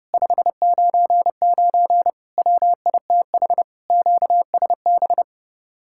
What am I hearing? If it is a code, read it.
Morse code, heard as a keyed tone, about 31 wpm, 700 Hz. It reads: 599 WITH QSB